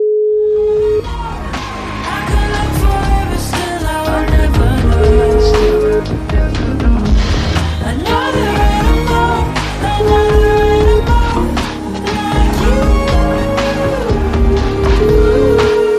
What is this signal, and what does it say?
Ringback tone (Europe, 425 Hz), with noise.